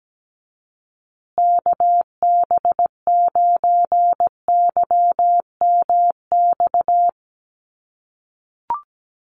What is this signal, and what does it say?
Morse code, heard as a keyed tone, about 17 wpm, 700 Hz. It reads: KB9YMX
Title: USA Call Signs Repeated 3x 17wpm